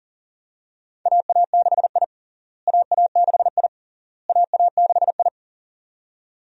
Morse code, heard as a keyed tone, about 40 wpm, 700 Hz. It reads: AA6I AA6I AA6I
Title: USA Call Signs - 4 Characters - Repeated 3x - 40wpm